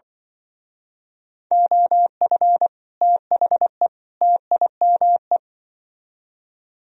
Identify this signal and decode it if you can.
Morse code, heard as a keyed tone, about 25 wpm, 700 Hz. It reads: OF THE TIME
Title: Speed-Racing - Sets of 3 Words 20wpm